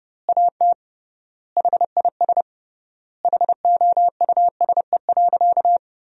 Morse code, heard as a keyed tone, about 30 wpm, 700 Hz. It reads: AT HIS HOUSE.
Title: 3x Word-Spacing - Sentences from Top 300 Words - 30wpm